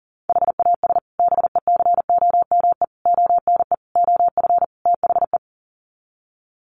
Morse code, heard as a keyed tone, about 40 wpm, 700 Hz. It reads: HAS BECOME ONE OF THE
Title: Sets of 5 Words Encore 40wpm